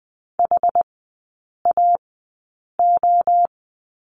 Morse code, heard as a keyed tone, about 20 wpm, spaced as for 8 wpm, 700 Hz. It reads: HAO
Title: 074 Lesson 13e - Letter L - Three Characters 20wpm